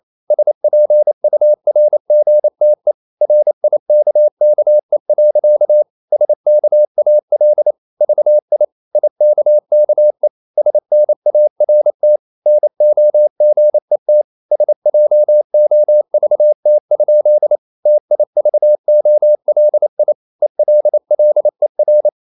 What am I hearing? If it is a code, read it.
Morse code, heard as a keyed tone, about 28 wpm, 600 Hz. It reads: SPURGTE RIKKE. SKAL VI IKKE SNART NOGET SJOVT? TIVOLI ELLER
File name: QRQ 28wpm